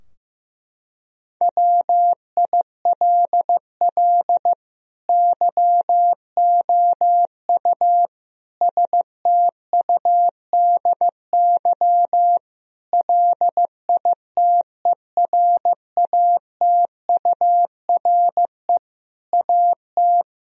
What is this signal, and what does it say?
Morse code, heard as a keyed tone, about 15 wpm, 700 Hz. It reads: WILL YOU STUDY LITERATURE AT
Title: Sentences from Top 1000 Words - Review Part 3 - 15wpm